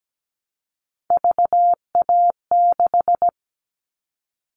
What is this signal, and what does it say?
Morse code, heard as a keyed tone, about 17 wpm, 700 Hz. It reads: VA6
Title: Callsign Prefixes for Top 25 DXCC - Repeated 3x - 17wpm